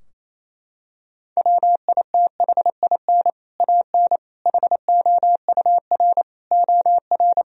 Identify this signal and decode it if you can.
Morse code, heard as a keyed tone, about 28 wpm, 700 Hz. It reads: WITHIN AN HOUR OR
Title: Sentences from Top 1000 Words - Review Part 4 - 28wpm